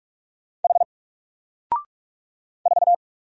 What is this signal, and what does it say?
Morse code, heard as a keyed tone, about 44 wpm, 700 Hz. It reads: H 4
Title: Single Letter-Number Alt 45wpm